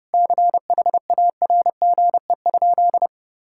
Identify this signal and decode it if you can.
Morse code, heard as a keyed tone, about 30 wpm, 700 Hz. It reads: CHARGE?
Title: Sentences from Top 1000 Words - Review Part 6 - 30wpm